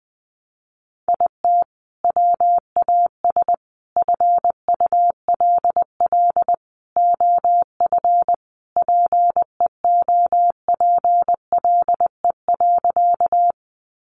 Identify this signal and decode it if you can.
Morse code, heard as a keyed tone, about 20 wpm, 700 Hz. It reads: IT WAS FULL OF PEOPLE.
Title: Sentences from Top 500 Words 20wpm